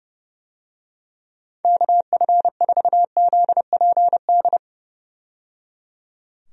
Morse code, heard as a keyed tone, about 29 wpm, 700 Hz. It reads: KF4ZPD
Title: USA Call Signs - 6 Characters - 30wpm